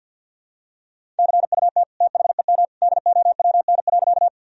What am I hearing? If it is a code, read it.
Morse code, heard as a keyed tone, about 46 wpm, 700 Hz. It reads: CUT THEM DOWN.
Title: Sentences from Top 1000 Words - Review Part 1 - 50wpm